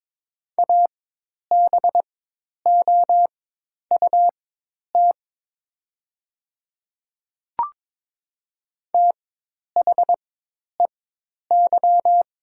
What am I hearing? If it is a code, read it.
Morse code, heard as a keyed tone, about 22 wpm, spaced as for 10 wpm, 700 Hz. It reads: ABOUT THEY
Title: 4x Character Spacing - Top 200 Words - 22wpm